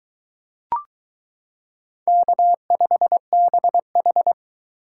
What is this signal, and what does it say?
Morse code, heard as a keyed tone, about 23 wpm, 700 Hz. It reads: K5BH